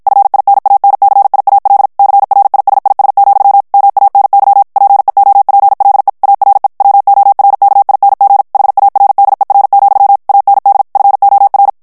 Morse code, heard as a keyed tone, about 55 wpm, 800 Hz. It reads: FINANCIAL CRISIS, MANY PEOPLE ARE WORKING HARDER, AND FOR
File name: morse mp3 / ANGLAIS texte clair 1 vitesse 50 mots minute